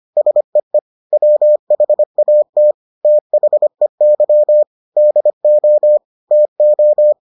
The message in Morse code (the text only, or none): SEE WHAT THEY DO TO